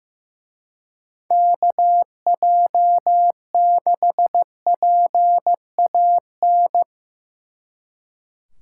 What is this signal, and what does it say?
Morse code, heard as a keyed tone, about 17 wpm, 700 Hz. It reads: KJ6PAN
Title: Speed Racing - USA 2x3 Call Signs 15wpm